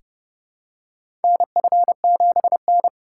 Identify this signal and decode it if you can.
Morse code, heard as a keyed tone, about 30 wpm, 700 Hz. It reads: NF7N
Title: USA Call Signs - 4 Characters - 3x Character Spacing - 30wpm